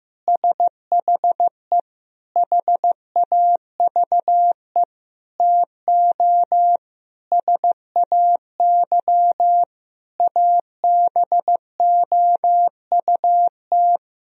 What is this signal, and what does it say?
Morse code, heard as a keyed tone, about 15 wpm, 700 Hz. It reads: SHE HAVE TO SAY ABOUT
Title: Sentences from Top 1000 Words - Review Part 3 - 15wpm